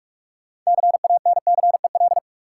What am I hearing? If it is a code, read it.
Morse code, heard as a keyed tone, about 46 wpm, 700 Hz. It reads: CANCEL
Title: Sentences from Top 1200 Words - New Word Drill - 45wpm